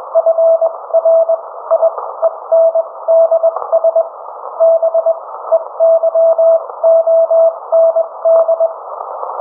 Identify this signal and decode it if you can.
Morse code, heard as a keyed tone, about 21 wpm, spaced as for 17 wpm, 660 Hz. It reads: FRIENDS BEYOND